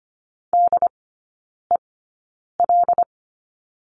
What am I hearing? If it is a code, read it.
Morse code, heard as a keyed tone, about 25 wpm, spaced as for 9 wpm, 700 Hz. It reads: DEL
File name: course / 25:10wpm / 093 Lesson 16e - Letter c - Three Characters 25wpm